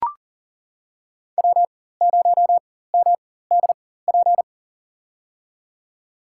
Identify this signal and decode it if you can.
Morse code, heard as a keyed tone, about 38 wpm, spaced as for 18 wpm, 700 Hz. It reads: W0MDP
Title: USA Call Signs - 5 Characters - 4x Character Spacing - 40wpm